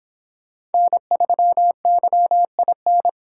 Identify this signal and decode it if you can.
Morse code, heard as a keyed tone, about 28 wpm, 700 Hz. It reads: N3YIN